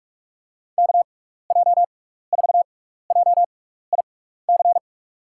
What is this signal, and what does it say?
Morse code, heard as a keyed tone, about 46 wpm, spaced as for 15 wpm, 700 Hz. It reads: KJ4JIC
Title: USA Call Signs - 6 Characters - 6x Character Spacing - 45wpm